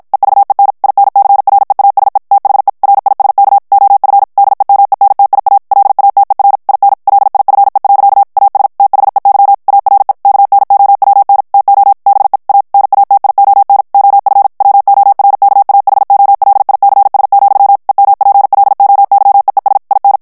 Morse code, heard as a keyed tone, about 50 wpm, 800 Hz. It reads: EVEN INCREASE THE RISK OF DEMENTIA LATER IN LIFE. AS THEY ARE KNOWN TO BE A NATION OF WORKAHOLICS, EMPLOYEES IN
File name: morse mp3 / ANGLAIS texte clair 1 vitesse 50 mots minute